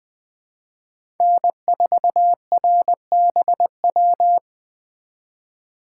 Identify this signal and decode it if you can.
Morse code, heard as a keyed tone, about 21 wpm, 700 Hz. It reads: N4RBW
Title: Speed Racing - USA 1x3 Call Signs 17wpm